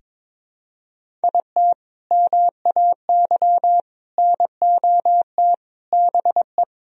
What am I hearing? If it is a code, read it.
Morse code, heard as a keyed tone, about 21 wpm, 700 Hz. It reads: IT MAY NOT BE
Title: Sentences from Top 1000 Words - Review Part 4 - 22wpm